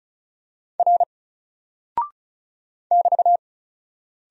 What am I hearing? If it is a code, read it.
Morse code, heard as a keyed tone, about 36 wpm, 700 Hz. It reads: R <BT>
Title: Single Character (Letter:Number:Punctuation:Prosign) Alt 35wpm